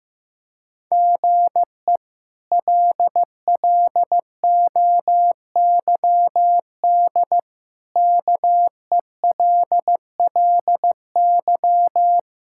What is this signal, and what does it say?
Morse code, heard as a keyed tone, about 15 wpm, 700 Hz. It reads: GE LLOYD KELLY
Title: K1USN SST QSO Practice - 15wpm